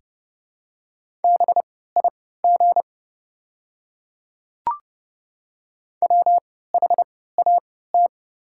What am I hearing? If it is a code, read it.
Morse code, heard as a keyed tone, about 30 wpm, spaced as for 17 wpm, 700 Hz. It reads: BIG WHAT
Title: 3x Character Spacing - Top 300 Words - 30wpm